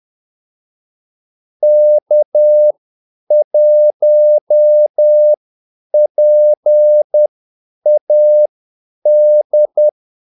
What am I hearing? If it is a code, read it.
Morse code, heard as a keyed tone, about 10 wpm, spaced as for 8 wpm, 600 Hz. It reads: K1PAD
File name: Callsigns-10-8wpm